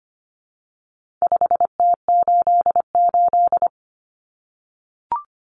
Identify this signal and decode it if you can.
Morse code, heard as a keyed tone, about 25 wpm, 700 Hz. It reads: 5T88
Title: Common Cut Numbers - 4-Digit 25wpm